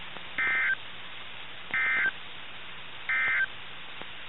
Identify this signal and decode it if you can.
Emergency Alert System (SAME) tones, heard as harsh buzzing bursts. It is heard from 0.3 to 3.5 s.
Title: NOAA Weather Radio - KIH61